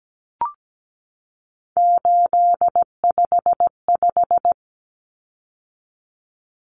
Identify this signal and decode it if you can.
Morse code, heard as a keyed tone, about 17 wpm, 700 Hz. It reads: 855
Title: Numbers - 3-Digit 17wpm